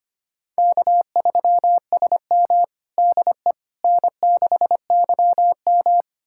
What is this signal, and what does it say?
Morse code, heard as a keyed tone, about 25 wpm, 700 Hz. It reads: K3SM DE N6YM
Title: USA Callsign DE Callsign - 4 Characters - 25wpm